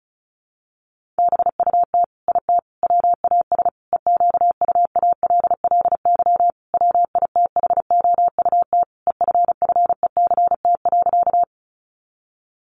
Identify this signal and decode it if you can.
Morse code, heard as a keyed tone, about 34 wpm, 700 Hz. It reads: BUT IT WAS EQUALLY WITHOUT EFFECT.
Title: Sentences from Top 1400 Words 35wpm